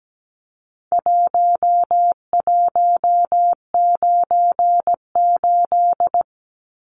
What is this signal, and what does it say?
Morse code, heard as a keyed tone, about 17 wpm, 700 Hz. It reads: 1198
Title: Numbers - Contest Style - 4-Digit 17wpm